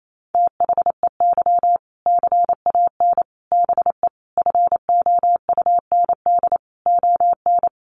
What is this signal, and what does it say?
Morse code, heard as a keyed tone, about 28 wpm, 700 Hz. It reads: THEY CAN BE FOUND ON
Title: Sentences from Top 1500 Words 28wpm